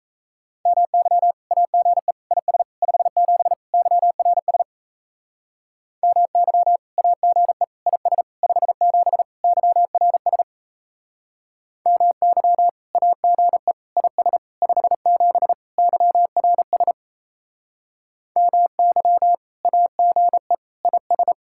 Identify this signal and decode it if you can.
Morse code, heard as a keyed tone, about 36 wpm, 700 Hz. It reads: MY AGE IS 57 YRS MY AGE IS 57 YRS MY AGE IS 57 YRS MY AGE IS